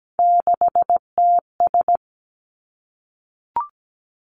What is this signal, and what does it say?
Morse code, heard as a keyed tone, about 17 wpm, 700 Hz. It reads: 6TS
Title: Three Challenging Letter-Number Combinations 17wpm